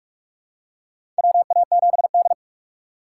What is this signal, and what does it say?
Morse code, heard as a keyed tone, about 46 wpm, 690 Hz. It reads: WA7D
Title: USA Call Signs - 4 Characters - 2x Character Spacing - 45wpm